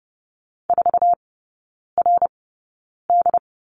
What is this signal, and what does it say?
Morse code, heard as a keyed tone, about 30 wpm, spaced as for 9 wpm, 700 Hz. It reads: 4RD
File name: course / 30:10wpm / 087 Lesson 15e - Number 5 - Three Characters 30wpm